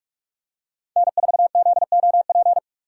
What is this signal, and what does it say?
Morse code, heard as a keyed tone, about 46 wpm, 700 Hz. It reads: N4ZOP
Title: USA Call Signs - 2x Character Spacing - 45wpm